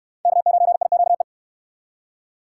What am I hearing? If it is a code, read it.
Morse code, heard as a keyed tone, about 68 wpm, 700 Hz. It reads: K9IQE